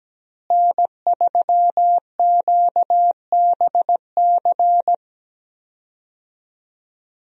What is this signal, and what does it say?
Morse code, heard as a keyed tone, about 17 wpm, 700 Hz. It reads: N3QBC
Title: USA Call Signs - 5 Characters - Repeated 2x - 17wpm